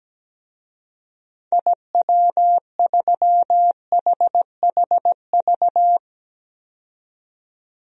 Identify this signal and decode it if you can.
Morse code, heard as a keyed tone, about 17 wpm, 700 Hz. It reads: IW3HHV
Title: DXCC-100 Call Signs - 6 Characters - 17wpm